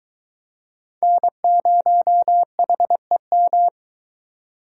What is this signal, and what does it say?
Morse code, heard as a keyed tone, about 20 wpm, 700 Hz. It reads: N0HEM